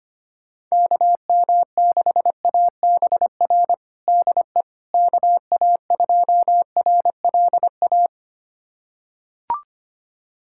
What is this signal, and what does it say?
Morse code, heard as a keyed tone, about 25 wpm, 700 Hz. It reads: KM6ABR DE KA2RLA
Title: USA Callsign DE Callsign 25wpm